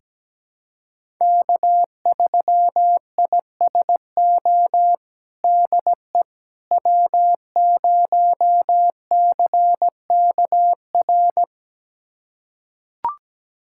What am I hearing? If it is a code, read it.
Morse code, heard as a keyed tone, about 17 wpm, 700 Hz. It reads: K3ISO DE W0CKR
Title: USA Callsign DE Callsign - 5 Characters - 17wpm